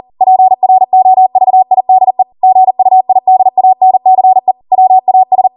Morse code, heard as a keyed tone, about 40 wpm, 750 Hz. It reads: PROVIDE GUIDANCE WAS